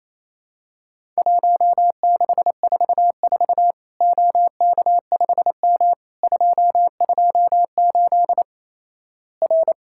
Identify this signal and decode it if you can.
Morse code, heard as a keyed tone, about 28 wpm, 700 Hz. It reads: 1644 OK5M 228 R
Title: Worked All Europe DX Contest - QTC Exchange - 28wpm